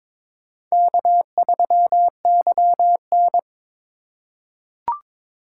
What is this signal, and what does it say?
Morse code, heard as a keyed tone, about 22 wpm, 700 Hz. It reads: K3YN
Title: USA Call Signs - 4 Characters - 2x Character Spacing - 22wpm